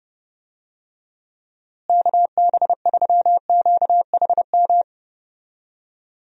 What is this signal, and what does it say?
Morse code, heard as a keyed tone, about 30 wpm, 700 Hz. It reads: KB3QHM
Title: USA Call Signs - 6 Characters - Repeated 3x - 30wpm